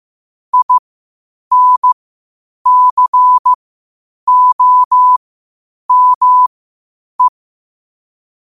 Morse code, heard as a keyed tone, about 15 wpm, spaced as for 8 wpm, 1000 Hz. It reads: INCOME
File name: qotdfiles / 8wpm